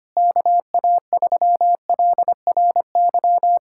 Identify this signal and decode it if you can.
Morse code, heard as a keyed tone, about 25 wpm, 700 Hz. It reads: KA3LRY
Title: USA Callsign DE Callsign - 6 Characters - 25wpm